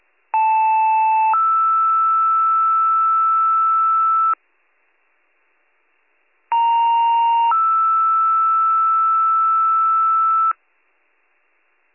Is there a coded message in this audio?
Yes: two-tone page from 0.3 to 10.5 s, heard as a one-second tone, then a three-second tone, about 880 then 1320 Hz, about 930 then 1320 Hz.